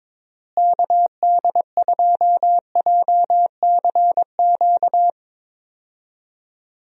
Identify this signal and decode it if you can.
Morse code, heard as a keyed tone, about 22 wpm, 700 Hz. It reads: KD2JCQ
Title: USA Call Signs - 6 Characters - Repeated 3x - 22wpm